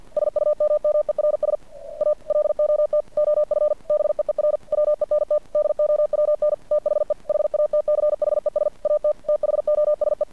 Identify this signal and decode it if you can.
Morse code, heard as a keyed tone, about 50 wpm, 600 Hz. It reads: SUMMERS A LOT OF BEER WENT DOWN THE HATCHES AT THOSE
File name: archived / CRSnet